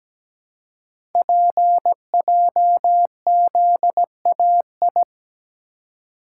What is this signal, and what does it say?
Morse code, heard as a keyed tone, about 17 wpm, 700 Hz. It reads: PJZAI
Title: Code Groups - 5 Letters - 17wpm